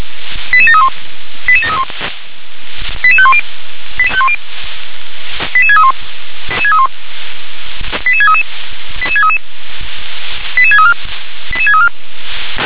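Five-tone selective call (ZVEI), heard as quick runs of five tones, tones of about 70 ms. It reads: 88731 88731 88510 88631 88633 88633 88634 88634
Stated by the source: ZVEI five-tone call received by SDR.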